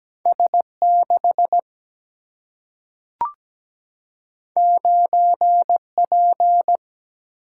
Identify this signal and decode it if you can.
Morse code, heard as a keyed tone, about 17 wpm, 700 Hz. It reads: S6 9P
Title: Single-Letter Single-Number 17wpm